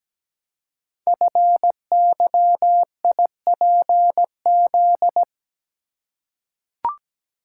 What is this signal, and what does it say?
Morse code, heard as a keyed tone, about 17 wpm, 700 Hz. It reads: FYIPZ
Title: Code Groups - 5 Letter:Numbers - 17wpm